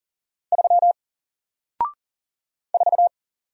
Morse code, heard as a keyed tone, about 40 wpm, 700 Hz. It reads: 3 4
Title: Single Character (Letter:Number:Punctuation:Prosign) Alt 40wpm